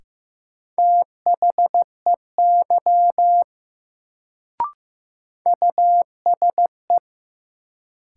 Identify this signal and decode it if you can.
Morse code, heard as a keyed tone, about 15 wpm, 700 Hz. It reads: THEY USE
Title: Sentences from Top 100 Words - New Word Drill - 15wpm